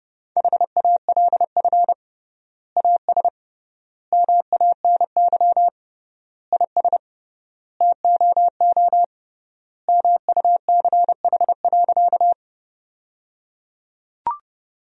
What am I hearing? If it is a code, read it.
Morse code, heard as a keyed tone, about 30 wpm, 700 Hz. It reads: HALF AS MANY IS TOO MUCH.
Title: 3x Word-Spacing - Sentences from Top 400 Words - 30wpm